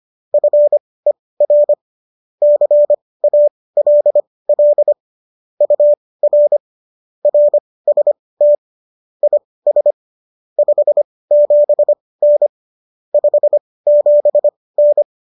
Morse code, heard as a keyed tone, about 25 wpm, spaced as for 18 wpm, 590 Hz. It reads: FER CALL UR RST IS 57N 57N